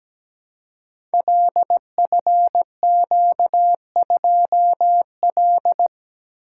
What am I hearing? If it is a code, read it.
Morse code, heard as a keyed tone, about 17 wpm, 700 Hz. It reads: LFQ2L
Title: Code Groups - 5 Letter:Numbers - 17wpm